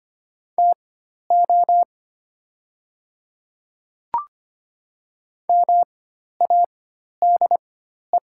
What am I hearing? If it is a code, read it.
Morse code, heard as a keyed tone, about 25 wpm, spaced as for 12 wpm, 700 Hz. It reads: TO MADE
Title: 4x Character Spacing - Top 200 Words - 25wpm